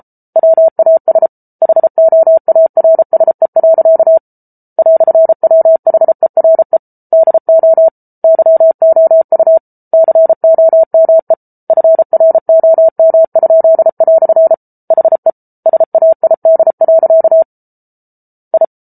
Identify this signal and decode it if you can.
Morse code, heard as a keyed tone, about 34 wpm, 650 Hz. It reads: WAS HOARSE. "WHERE DO YOU COME FROM?" HE SAID. I